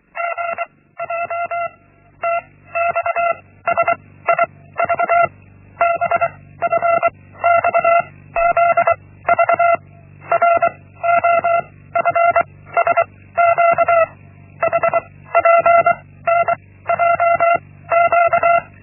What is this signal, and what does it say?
Morse code, heard as a keyed tone, about 24 wpm, 680 Hz. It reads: G J T X S I V B F X Z V R O F S Q H P N J Q
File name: Linksmakalnis-1